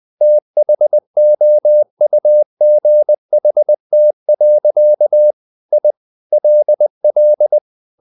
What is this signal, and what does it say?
Morse code, heard as a keyed tone, about 20 wpm, 600 Hz. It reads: THOUGHT. I LL
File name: the old man and the sea part 2 20wpm